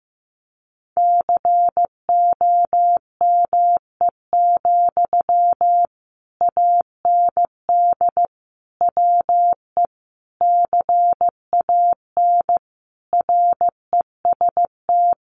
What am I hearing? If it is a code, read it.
Morse code, heard as a keyed tone, about 15 wpm, 700 Hz. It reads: COME, AND WE CAN REST
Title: Sentences from Top 500 Words 15wpm